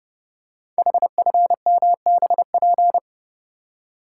Morse code, heard as a keyed tone, about 30 wpm, 700 Hz. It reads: HFMBP
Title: Code Groups - 5 Letter:Numbers - 30wpm